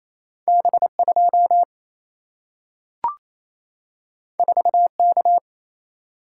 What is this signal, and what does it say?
Morse code, heard as a keyed tone, about 28 wpm, 700 Hz. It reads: B2 4K
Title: Single-Letter Single-Number 28wpm